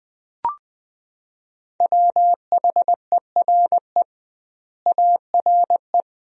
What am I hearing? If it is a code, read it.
Morse code, heard as a keyed tone, about 20 wpm, 700 Hz. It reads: WHERE ARE
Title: 2x Word-Spacing - Sentences from Top 400 Words - 20wpm